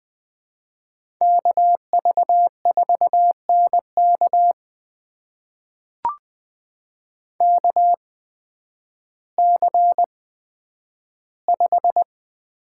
Morse code, heard as a keyed tone, about 20 wpm, spaced as for 11 wpm, 700 Hz. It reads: KV4NK KC5
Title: USA Call Signs - 5 Characters - 8x Character Spacing - 20wpm